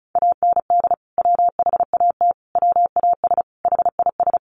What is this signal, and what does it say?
Morse code, heard as a keyed tone, about 34 wpm, 700 Hz. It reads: AND WHAT WAS HIS
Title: Sentences from Top 200 Words 35wpm